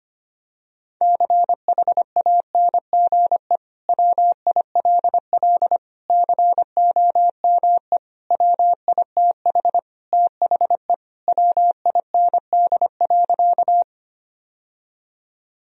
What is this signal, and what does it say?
Morse code, heard as a keyed tone, about 25 wpm, 700 Hz. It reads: CHANGE WILL COME WITH THE WIND.
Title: Sentences from Top 1000 Words - Review Part 2 - 25wpm